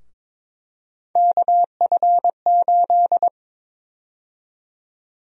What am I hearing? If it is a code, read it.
Morse code, heard as a keyed tone, about 22 wpm, 700 Hz. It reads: KF8
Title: USA Call Sign Prefixes - 22wpm